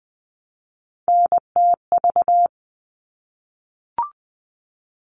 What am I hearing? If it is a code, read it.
Morse code, heard as a keyed tone, about 20 wpm, 700 Hz. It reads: NTV
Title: All Cut Numbers - 3-Digit 20wpm